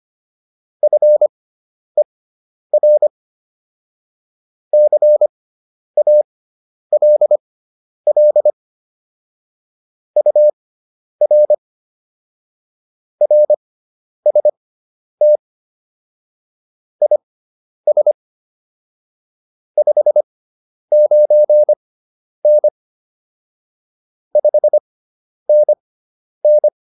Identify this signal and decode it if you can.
Morse code, heard as a keyed tone, about 25 wpm, spaced as for 10 wpm, 600 Hz. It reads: FER CALL UR RST IS 59N 5NN
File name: QSO107 10